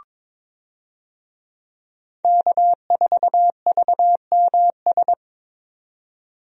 Morse code, heard as a keyed tone, about 22 wpm, 700 Hz. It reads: K4VMS
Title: USA Call Signs - 5 Characters - 22wpm